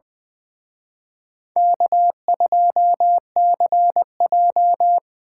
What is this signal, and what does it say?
Morse code, heard as a keyed tone, about 23 wpm, 700 Hz. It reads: K2CJ
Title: Speed Racing - USA 1x2 Call Signs 20wpm